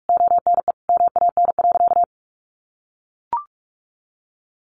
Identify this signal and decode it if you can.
Morse code, heard as a keyed tone, about 44 wpm, 700 Hz. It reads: ONE MAN.
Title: Sentences from Top 1500 Words 45wpm